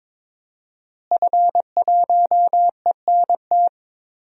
Morse code, heard as a keyed tone, about 22 wpm, 700 Hz. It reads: F1ENT
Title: DXCC-100 Call Signs - 5 Characters - Repeated 2x - 22wpm